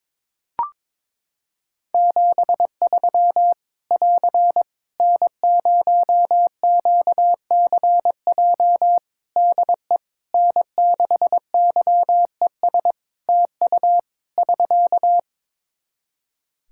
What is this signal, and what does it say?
Morse code, heard as a keyed tone, about 22 wpm, 700 Hz. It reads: 73 <AR> N0QCJ DE N6YES TU <SK>
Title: K1USN SST QSO Practice 22wpm